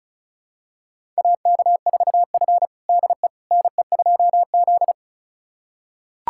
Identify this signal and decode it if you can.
Morse code, heard as a keyed tone, about 36 wpm, 700 Hz. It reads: AK4F DE NE2Z
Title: USA Callsign DE Callsign - 4 Characters - 35wpm